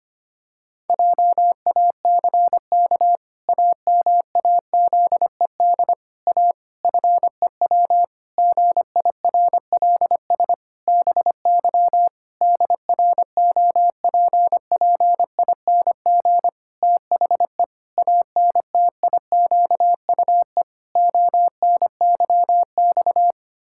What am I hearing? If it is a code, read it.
Morse code, heard as a keyed tone, about 25 wpm, 700 Hz. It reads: JACK AMAZED A FEW GIRLS BY DROPPING THE ANTIQUE ONYX
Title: Pangram Challenge v2 - 25wpm